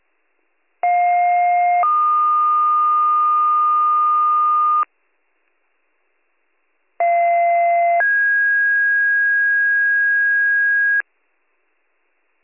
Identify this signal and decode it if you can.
Two-tone page, heard as a one-second tone, then a three-second tone, about 710 then 1220 Hz, about 690 then 1650 Hz. It is heard from 0.8 to 11.0 s.